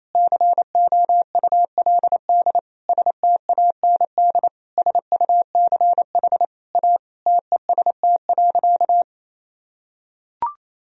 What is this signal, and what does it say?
Morse code, heard as a keyed tone, about 28 wpm, 700 Hz. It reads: COULD STAND SUCH A TEST.
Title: Sentences from Top 1000 Words - Review Part 3 - 28wpm